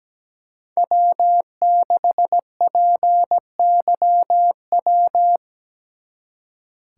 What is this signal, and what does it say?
Morse code, heard as a keyed tone, about 21 wpm, 700 Hz. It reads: W6PYW
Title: Speed Racing - USA 1x3 Call Signs 17wpm